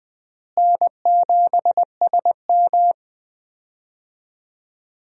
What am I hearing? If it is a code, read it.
Morse code, heard as a keyed tone, about 20 wpm, 700 Hz. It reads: N7SM
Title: USA Call Signs - 4 Characters - Repeated 2x - 20wpm